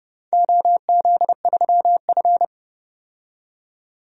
Morse code, heard as a keyed tone, about 32 wpm, 700 Hz. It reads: OZ3F
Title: DXCC-100 Call Signs - 4 Characters - 30wpm